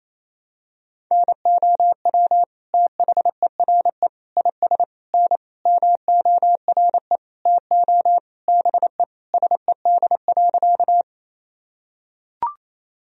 Morse code, heard as a keyed tone, about 28 wpm, 700 Hz. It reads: NOW THERE IS N MORE TO BE SED.
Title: Sentences from Top 100 Words and Abbreviations 28wpm